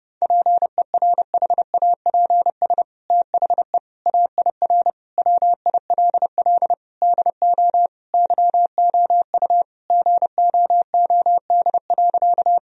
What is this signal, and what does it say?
Morse code, heard as a keyed tone, about 32 wpm, 700 Hz. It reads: PERHAPS THE AIR WILL DO YOU GOOD.
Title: Sentences from Top 1000 Words - Review Part 1 - 30wpm